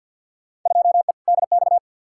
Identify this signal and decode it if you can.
Morse code, heard as a keyed tone, about 51 wpm, 700 Hz. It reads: 2E DX
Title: Field Day Exchanges - 50wpm